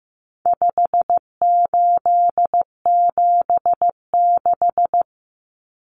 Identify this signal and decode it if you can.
Morse code, heard as a keyed tone, about 15 wpm, 700 Hz. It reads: 5876
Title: Numbers - 4-Digit 15wpm